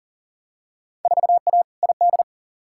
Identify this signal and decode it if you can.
Morse code, heard as a keyed tone, about 40 wpm, 700 Hz. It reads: 4A ID
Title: Field Day Exchanges - 40wpm